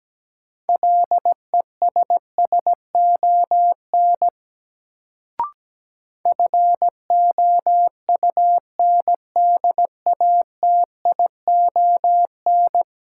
Morse code, heard as a keyed tone, about 17 wpm, 700 Hz. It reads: LESSON FOUNDATION
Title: Sentences from Top 1200 Words - New Word Drill - 17wpm